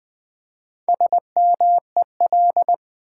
Morse code, heard as a keyed tone, about 20 wpm, 700 Hz. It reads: SMEL
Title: Build A Word - 1 Letter At A Time - 5-Letter Words - 20wpm